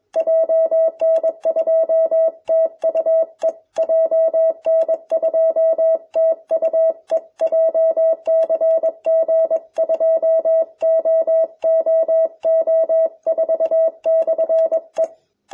Morse code, heard as a keyed tone, about 22 wpm, 660 Hz. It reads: JN2TUE JN2TUE JCG2OOO4/E